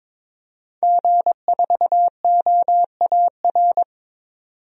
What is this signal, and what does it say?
Morse code, heard as a keyed tone, about 21 wpm, 700 Hz. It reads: G4OAR
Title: DXCC-100 Call Signs - 5 Characters - 22wpm